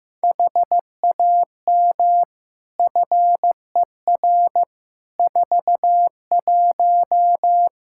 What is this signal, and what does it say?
Morse code, heard as a keyed tone, about 15 wpm, 700 Hz. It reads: HAM FER 41
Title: Practice QSO - Exchange 3 - More Info - 15wpm